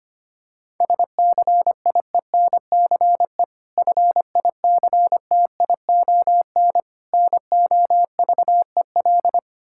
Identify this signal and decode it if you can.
Morse code, heard as a keyed tone, about 25 wpm, 700 Hz. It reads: SCIENCE FICTION NOVEL
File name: practice / 25wpm / Difficult Words with Hints - 25wpm